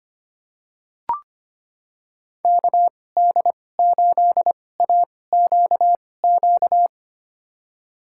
Morse code, heard as a keyed tone, about 25 wpm, spaced as for 18 wpm, 700 Hz. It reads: KD8AQQ
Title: USA Call Signs - 6 Characters - 2x Character Spacing - 25wpm